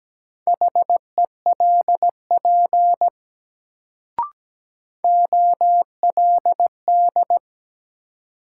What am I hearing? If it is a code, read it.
Morse code, heard as a keyed tone, about 17 wpm, 700 Hz. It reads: HELP OLD
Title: Sets of 2 Words - New Word Drill - 17wpm